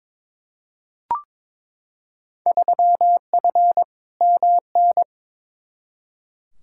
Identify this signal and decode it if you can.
Morse code, heard as a keyed tone, about 22 wpm, 700 Hz. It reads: 3F MN
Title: Field Day Exchanges Encore - 22wpm